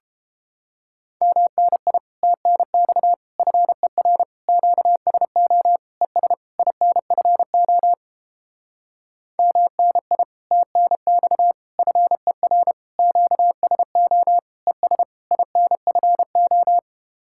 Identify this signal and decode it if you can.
Morse code, heard as a keyed tone, about 31 wpm, 700 Hz. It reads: MNI TNX FER QSO ES INFO MNI TNX FER QSO ES INFO